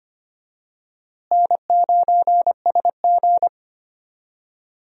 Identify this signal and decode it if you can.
Morse code, heard as a keyed tone, about 25 wpm, 700 Hz. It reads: N9SG
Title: USA Call Signs - 4 Characters - Repeated 3x - 25wpm